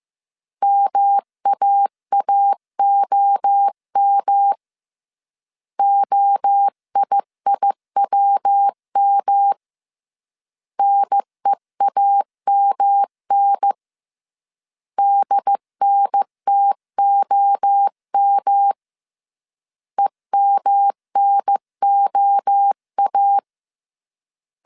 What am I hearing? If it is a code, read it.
Morse code, heard as a keyed tone, about 15 wpm, 800 Hz. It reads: MAAOM OIIWM NEAMN DNTOM EMNOA
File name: CW / DET / 1 3 1500C S C 14wpm 2458